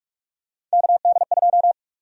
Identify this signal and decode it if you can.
Morse code, heard as a keyed tone, about 44 wpm, 700 Hz. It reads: KD2
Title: USA Call Sign Prefixes - 45wpm